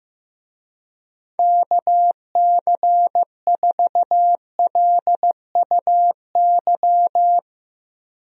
Morse code, heard as a keyed tone, about 15 wpm, 700 Hz. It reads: KC4LUY
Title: USA Call Signs - 6 Characters - Repeated 3x - 15wpm